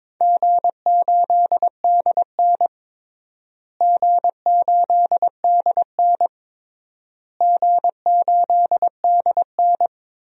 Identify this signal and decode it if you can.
Morse code, heard as a keyed tone, about 22 wpm, 700 Hz. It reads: G8DN G8DN G8DN
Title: DXCC-100 Call Signs - 4 Characters - Repeated 3x - 22wpm